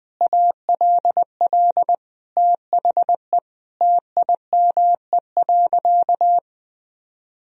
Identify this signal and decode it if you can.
Morse code, heard as a keyed tone, about 21 wpm, 700 Hz. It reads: ALL THE TIME.
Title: Sentences from Top 1000 Words - Review Part 3 - 20wpm